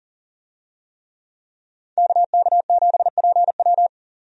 Morse code, heard as a keyed tone, about 40 wpm, 700 Hz. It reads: KK7PW
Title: USA Call Signs - 5 Characters - 40wpm